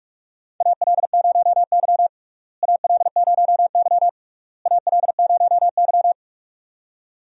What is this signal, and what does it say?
Morse code, heard as a keyed tone, about 41 wpm, 700 Hz. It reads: AL0Y AL0Y AL0Y
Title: USA Call Signs - 4 Characters - Repeated 3x - 45wpm